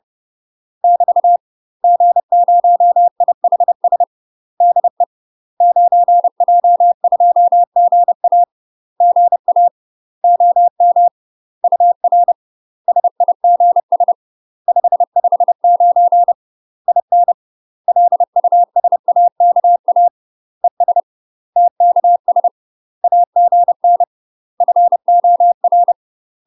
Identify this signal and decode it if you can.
Morse code, heard as a keyed tone, about 30 wpm, 700 Hz. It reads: <BT> G0IHS DE 9J2GA GA OM UR SIGS 559 IN LUSAKA ES TKS AGN FOR
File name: UK MASTER QSOs TRAINING / 30WPM / UK MASTER QSOs 0024